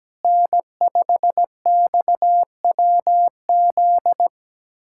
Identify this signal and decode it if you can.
Morse code, heard as a keyed tone, about 17 wpm, 700 Hz. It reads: N5XWZ
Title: USA Callsign DE Callsign - 5 Characters - 17wpm